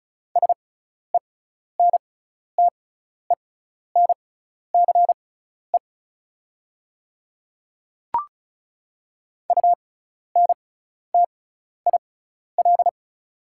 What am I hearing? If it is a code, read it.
Morse code, heard as a keyed tone, about 36 wpm, spaced as for 12 wpm, 690 Hz. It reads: SENTENCE UNTIL
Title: 6x Character Spacing - Top 300 Words - 35wpm